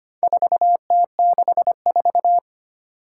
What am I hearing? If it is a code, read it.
Morse code, heard as a keyed tone, about 25 wpm, 700 Hz. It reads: 4T64
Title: Four Challenging Letter-Number Combinations 25wpm